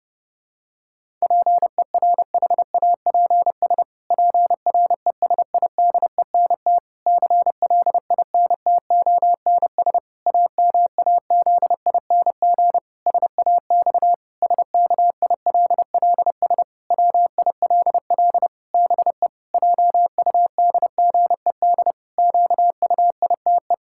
Morse code, heard as a keyed tone, about 30 wpm, 700 Hz. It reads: PERHAPS PRESIDENT CLINTONS AMAZING SAX SKILLS WILL BE JUDGED QUITE
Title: Pangram Challenge v2 - 30wpm